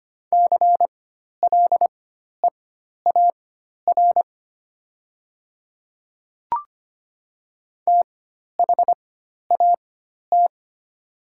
Morse code, heard as a keyed tone, about 25 wpm, spaced as for 12 wpm, 690 Hz. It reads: CLEAR THAT
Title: 4x Character Spacing - Top 500 Words - 25wpm